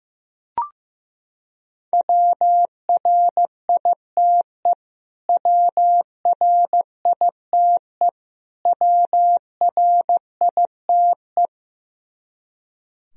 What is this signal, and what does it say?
Morse code, heard as a keyed tone, about 15 wpm, 700 Hz. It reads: WRITE WRITE WRITE
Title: Top 100 Words Repeated 15wpm